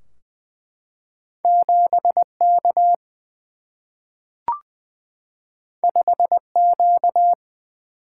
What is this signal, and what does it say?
Morse code, heard as a keyed tone, about 20 wpm, 700 Hz. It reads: 7K 5Q
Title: Single-Letter Single-Number 20wpm